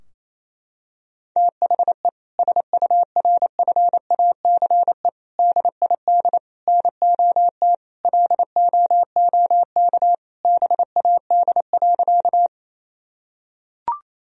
Morse code, heard as a keyed tone, about 28 wpm, 700 Hz. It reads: THE SURFACE DID NOT LOOK BAD.
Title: Sentences from Top 1000 Words - Review Part 3 - 28wpm